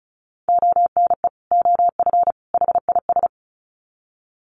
Morse code, heard as a keyed tone, about 34 wpm, 700 Hz. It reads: ONE OF HIS
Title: Sets of 3 Words 35wpm